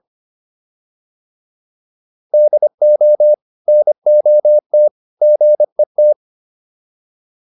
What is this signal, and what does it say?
Morse code, heard as a keyed tone, about 25 wpm, 600 Hz. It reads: DO NOT GET